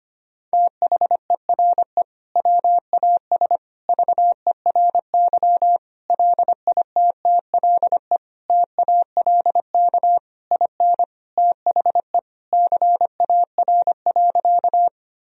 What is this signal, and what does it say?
Morse code, heard as a keyed tone, about 25 wpm, 700 Hz. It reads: THERE WAS VERY LITTLE TALK IN THE CAR.
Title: Sentences from Top 1000 Words - Review Part 1 - 25wpm